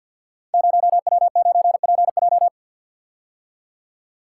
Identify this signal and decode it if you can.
Morse code, heard as a keyed tone, about 47 wpm, 700 Hz. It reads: 0W9PJ
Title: Code Groups - 5 Letter:Numbers - 50wpm